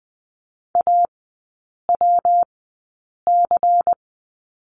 Morse code, heard as a keyed tone, about 20 wpm, spaced as for 8 wpm, 700 Hz. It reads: AWC
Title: 133 Lesson 22e - Question Mark - Three Characters 20wpm